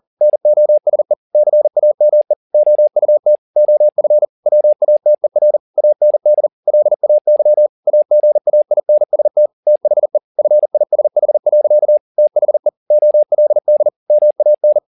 Morse code, heard as a keyed tone, about 40 wpm, 600 Hz. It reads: NOSE CAME OUT OF WATER AND LAY AGAINST THE FISH. THE OLD MAN
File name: the old man and the sea 2 40wpm